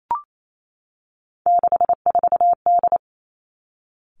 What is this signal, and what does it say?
Morse code, heard as a keyed tone, about 28 wpm, 700 Hz. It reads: 64D
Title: Three Challenging Letter-Number Combinations 28wpm